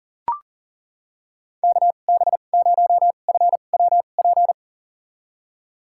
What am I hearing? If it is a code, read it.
Morse code, heard as a keyed tone, about 39 wpm, spaced as for 29 wpm, 700 Hz. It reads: KB0FWP
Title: USA Call Signs - 6 Characters - 2x Character Spacing - 40wpm